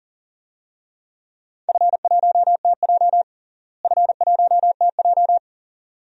Morse code, heard as a keyed tone, about 37 wpm, 700 Hz. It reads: F1TJ F1TJ
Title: DXCC-100 Call Signs - 4 Characters - Repeated 2x - 40wpm